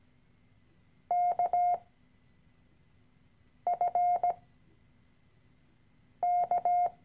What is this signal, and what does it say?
Morse code, heard as a keyed tone, about 17 wpm, spaced as for 4 wpm, 710 Hz. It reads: KFK
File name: CW-Online-Kurs / Anfänger / Lektion 013